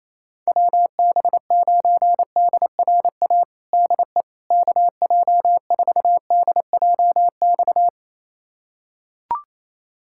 Morse code, heard as a keyed tone, about 28 wpm, 700 Hz. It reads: WB9DRA DE KJ4DJX
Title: USA Callsign DE Callsign - 6 Characters - 28wpm